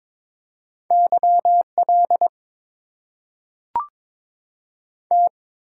Morse code, heard as a keyed tone, about 22 wpm, 690 Hz. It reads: YL T
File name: Top 90 QSO Elements - 8x Character Spacing - 22wpm